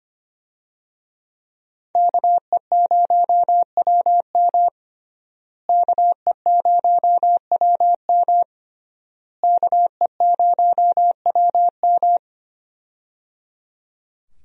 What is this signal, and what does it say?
Morse code, heard as a keyed tone, about 25 wpm, 700 Hz. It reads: KE0WM KE0WM KE0WM
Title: USA Call Signs - 5 Characters - Repeated 3x - 25wpm